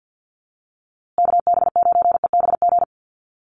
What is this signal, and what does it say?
Morse code, heard as a keyed tone, about 49 wpm, 700 Hz. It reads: KB9EBG
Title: USA Call Signs Repeated 2x 50wpm